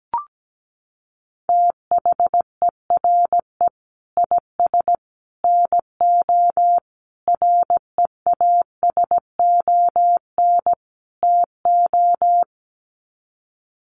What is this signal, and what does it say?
Morse code, heard as a keyed tone, about 17 wpm, 700 Hz. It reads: THERE IS NO REASON TO
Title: Sets of 5 Words Encore 17wpm